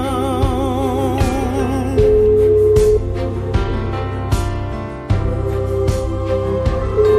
Ringback tone (Europe, 425 Hz), with noise.